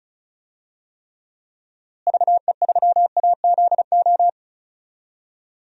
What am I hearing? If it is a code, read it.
Morse code, heard as a keyed tone, about 36 wpm, 700 Hz. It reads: VE3AZO
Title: DXCC-100 Call Signs - 6 Characters - Repeated 2x - 35wpm